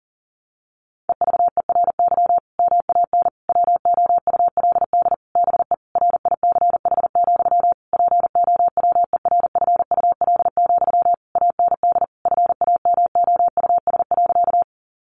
Morse code, heard as a keyed tone, about 40 wpm, 700 Hz. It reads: EVERY MAN WOULD BE RICH, POWERFUL, AND FAMOUS.
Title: Sentences from Top 1500 Words 40wpm